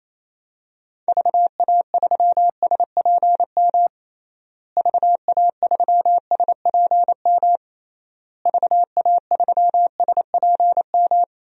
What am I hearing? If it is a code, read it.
Morse code, heard as a keyed tone, about 28 wpm, 700 Hz. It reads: VA3SPM VA3SPM VA3SPM
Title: DXCC-100 Call Signs - 6 Characters - Repeated 3x - 28wpm